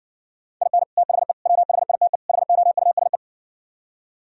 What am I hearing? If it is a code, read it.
Morse code, heard as a keyed tone, about 59 wpm, 700 Hz. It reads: IN THE WHITE HOUSE